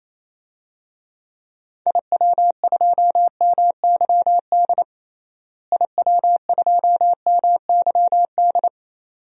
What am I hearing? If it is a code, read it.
Morse code, heard as a keyed tone, about 27 wpm, 700 Hz. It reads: IW2MYD IW2MYD
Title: DXCC-100 Call Signs - 6 Characters - Repeated 2x - 28wpm